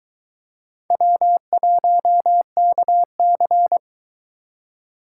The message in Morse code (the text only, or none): W1KC